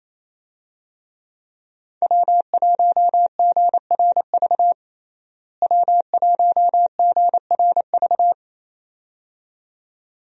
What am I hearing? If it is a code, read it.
Morse code, heard as a keyed tone, about 28 wpm, 700 Hz. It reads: W1GRV W1GRV
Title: USA Call Signs - 5 Characters - Repeated 2x - 28wpm